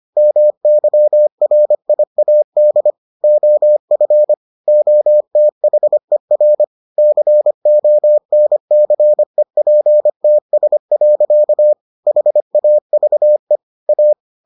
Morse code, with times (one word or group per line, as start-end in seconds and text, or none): 0.2-2.9 MYRIAD
3.2-4.3 OF
4.7-6.6 OTHER
7.0-11.7 CONCEPTS.
12.1-13.6 HAVE
13.9-14.1 A